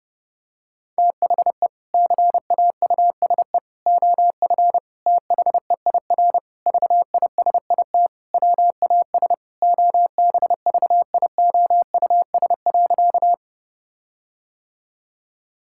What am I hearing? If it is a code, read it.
Morse code, heard as a keyed tone, about 29 wpm, 700 Hz. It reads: THE CAUSE OF THEIR VISIT WAS OBVIOUS.
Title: Sentences from Top 1600 Words 30wpm